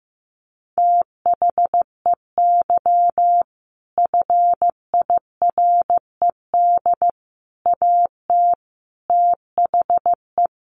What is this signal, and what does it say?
Morse code, heard as a keyed tone, about 15 wpm, 700 Hz. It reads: THEY FIRED AT THE
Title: A Sound Of Thunder 15wpm